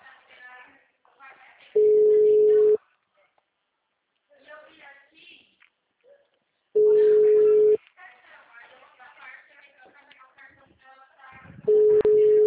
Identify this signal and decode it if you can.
Ringback tone (Europe, 425 Hz), with speech.
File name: out- -2001- - - .93998